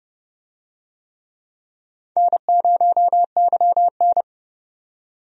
Morse code, heard as a keyed tone, about 32 wpm, 700 Hz. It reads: N0YN
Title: USA Call Signs - 4 Characters - 30wpm